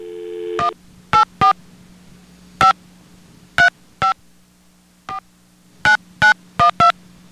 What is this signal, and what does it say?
DTMF, heard as telephone touch tones, tones of about 120 ms. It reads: 184 5 35 49912